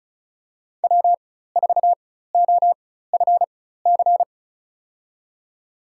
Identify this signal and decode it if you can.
Morse code, heard as a keyed tone, about 34 wpm, spaced as for 16 wpm, 700 Hz. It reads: W4OFC
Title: USA Call Signs - 5 Characters - 4x Character Spacing - 35wpm